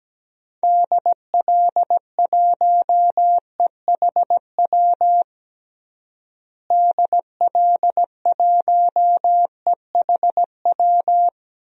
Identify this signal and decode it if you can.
Morse code, heard as a keyed tone, about 17 wpm, 700 Hz. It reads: DL1EHW DL1EHW
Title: DXCC-100 Call Signs - 6 Characters - Repeated 2x - 17wpm